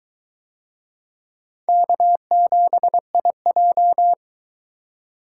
Morse code, heard as a keyed tone, about 20 wpm, 700 Hz. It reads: K7IJ